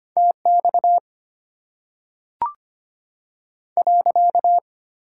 Morse code, heard as a keyed tone, about 25 wpm, 700 Hz. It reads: T X .
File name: Top 90 QSO Elements - 8x Character Spacing - 25wpm